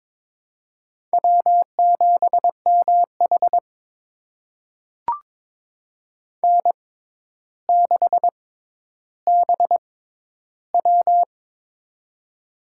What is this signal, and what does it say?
Morse code, heard as a keyed tone, about 22 wpm, spaced as for 11 wpm, 700 Hz. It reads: W7MH N6BW
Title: USA Call Signs - 6x Character Spacing - 22wpm